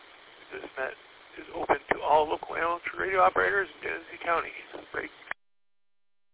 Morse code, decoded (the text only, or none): none